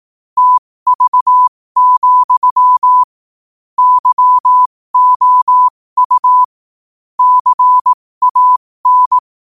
Morse code, heard as a keyed tone, about 18 wpm, spaced as for 15 wpm, 1000 Hz. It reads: TV, YOU CAN
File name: qotdfiles / 15wpm